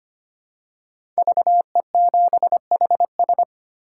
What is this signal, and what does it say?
Morse code, heard as a keyed tone, about 25 wpm, 700 Hz. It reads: VE7HS
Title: K1USN SST QSO Practice - 25wpm